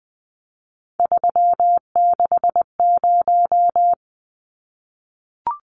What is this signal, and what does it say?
Morse code, heard as a keyed tone, about 20 wpm, 700 Hz. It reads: 360
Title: Numbers - 3-Digit 20wpm